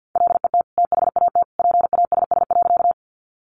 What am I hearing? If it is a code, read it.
Morse code, heard as a keyed tone, about 50 wpm, 700 Hz. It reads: LET THAT PASS.
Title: Sentences from Top 400 Words 50wpm